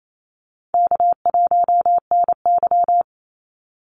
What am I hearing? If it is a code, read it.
Morse code, heard as a keyed tone, about 28 wpm, 700 Hz. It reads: K1NY
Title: Easy Call Signs Repeated 2x - 28wpm